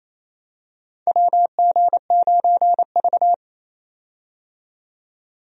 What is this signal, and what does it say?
Morse code, heard as a keyed tone, about 28 wpm, 700 Hz. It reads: WG9V
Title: USA Call Signs - 4 Characters - 28wpm